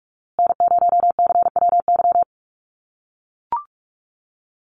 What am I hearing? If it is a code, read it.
Morse code, heard as a keyed tone, about 44 wpm, 700 Hz. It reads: N0CWY
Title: USA Call Signs Repeated 3x 45wpm